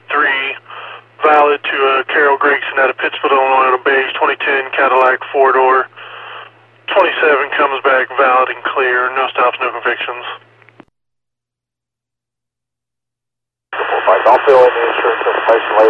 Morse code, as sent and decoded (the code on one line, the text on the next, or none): none